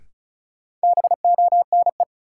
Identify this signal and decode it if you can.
Morse code, heard as a keyed tone, about 36 wpm, 700 Hz. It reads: BONE
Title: Sentences from Top 1600 Words - New Word Drill Alt - 35wpm